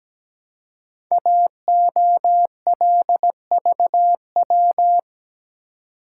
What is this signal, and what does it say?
Morse code, heard as a keyed tone, about 17 wpm, 700 Hz. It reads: AOLVW
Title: Code Groups - 5 Letters - 17wpm